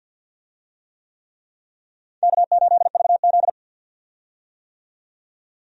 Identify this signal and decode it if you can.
Morse code, heard as a keyed tone, about 48 wpm, 700 Hz. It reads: K8VZ
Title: USA Call Signs - 4 Characters - 50wpm